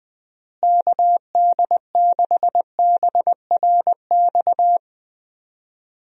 Morse code, heard as a keyed tone, about 20 wpm, 700 Hz. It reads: KD6BRX
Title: USA Call Signs - 6 Characters - Repeated 3x - 20wpm